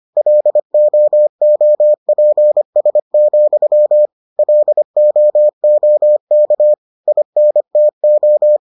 Morse code, heard as a keyed tone, about 25 wpm, 600 Hz. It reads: LOOPS, LOOK INTO